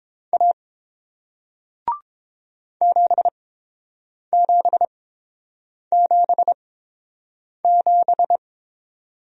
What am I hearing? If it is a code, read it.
Morse code, heard as a keyed tone, about 30 wpm, spaced as for 7 wpm, 700 Hz. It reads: A 7777